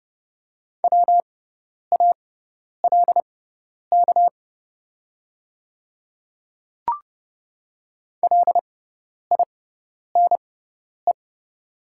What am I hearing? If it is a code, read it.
Morse code, heard as a keyed tone, about 30 wpm, spaced as for 10 wpm, 700 Hz. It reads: WALK LINE
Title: 6x Character Spacing - Top 300 Words - 30wpm